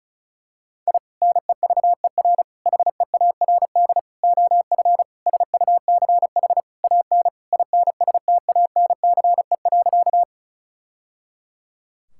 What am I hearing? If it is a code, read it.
Morse code, heard as a keyed tone, about 36 wpm, 690 Hz. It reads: I NEVER HEARD OF SUCH AN INSTANCE.
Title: Sentences from Top 1000 Words - Review Part 5 - 35wpm